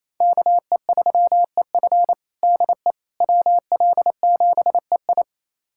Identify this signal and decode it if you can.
Morse code, heard as a keyed tone, about 28 wpm, 700 Hz. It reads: KE3EF DE WL7EI
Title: USA Callsign DE Callsign - 5 Characters - 28wpm